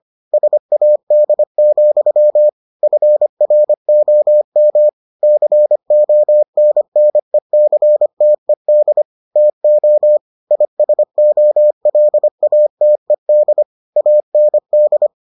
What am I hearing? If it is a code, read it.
Morse code, heard as a keyed tone, about 25 wpm, 600 Hz. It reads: SAD, FROM CONNECTED TO ISOLATED AND